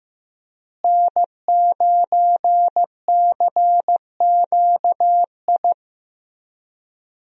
Morse code, heard as a keyed tone, about 15 wpm, 700 Hz. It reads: N9CQI
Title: USA Call Signs - 5 Characters - Repeated 2x - 15wpm